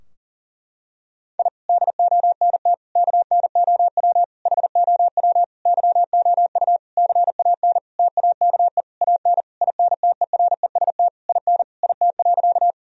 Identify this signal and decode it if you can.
Morse code, heard as a keyed tone, about 40 wpm, 700 Hz. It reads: I DONT KNOW HOW YOU CAN TAKE AN INTEREST IN IT.
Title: Sentences from Top 1000 Words - Review Part 2 - 40wpm